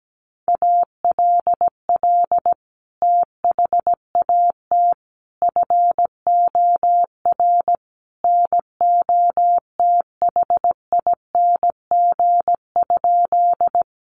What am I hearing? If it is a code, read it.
Morse code, heard as a keyed tone, about 17 wpm, 700 Hz. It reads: ALL THAT FOR NOTHING?
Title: Sentences from Top 500 Words 17wpm